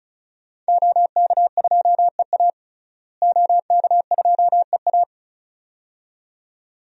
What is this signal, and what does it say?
Morse code, heard as a keyed tone, about 34 wpm, 700 Hz. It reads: OK2EA OK2EA
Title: DXCC-100 Call Signs - 5 Characters - Repeated 2x - 35wpm